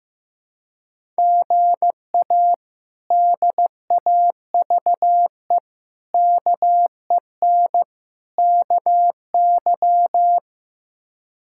Morse code, heard as a keyed tone, about 15 wpm, 700 Hz. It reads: GA DAVE KEN KY
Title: K1USN SST QSO Practice - 15wpm